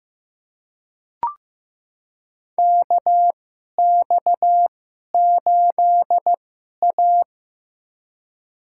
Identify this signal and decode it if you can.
Morse code, heard as a keyed tone, about 15 wpm, spaced as for 11 wpm, 700 Hz. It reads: KX8A
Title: USA Call Signs - 4 Characters - 2x Character Spacing - 15wpm